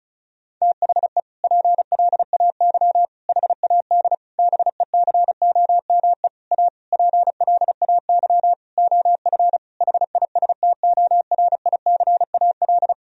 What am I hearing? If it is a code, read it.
Morse code, heard as a keyed tone, about 36 wpm, 700 Hz. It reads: THE PLAY HAD BECOME A PLAY OF HISTORICAL
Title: Sentences from Top 1000 Words - Review Part 4 - 35wpm